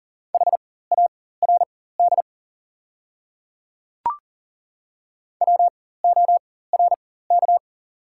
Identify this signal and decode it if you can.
Morse code, heard as a keyed tone, about 37 wpm, spaced as for 18 wpm, 690 Hz. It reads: HARD WORK
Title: 4x Character Spacing - Top 300 Words - 40wpm